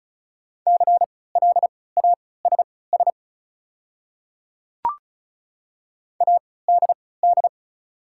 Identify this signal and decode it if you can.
Morse code, heard as a keyed tone, about 36 wpm, spaced as for 20 wpm, 700 Hz. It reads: CLASS ADD
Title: 3x Character Spacing - Top 500 Words - 35wpm